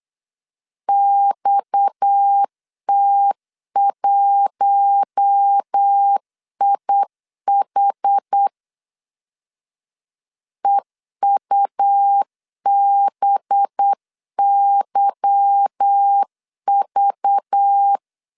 Morse code, heard as a keyed tone, about 8 wpm, 800 Hz. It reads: XT1IH EUBYV
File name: LEZIONI / 8 YJ45 / 8 Y 8wpm 3245